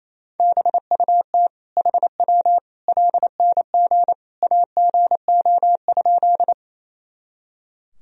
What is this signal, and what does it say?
Morse code, heard as a keyed tone, about 28 wpm, 700 Hz. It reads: BUT HW LNG AGO?
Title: Sentences from Top 500 Words and Abbreviations 28wpm